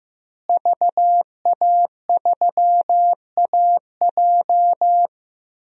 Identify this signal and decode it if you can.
Morse code, heard as a keyed tone, about 15 wpm, 700 Hz. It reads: VA3AJ
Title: DXCC-100 Call Signs - 5 Characters - Repeated 3x - 15wpm